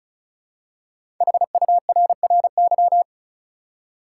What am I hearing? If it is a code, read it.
Morse code, heard as a keyed tone, about 34 wpm, 700 Hz. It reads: HURRY
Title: Five Letter Words - 35wpm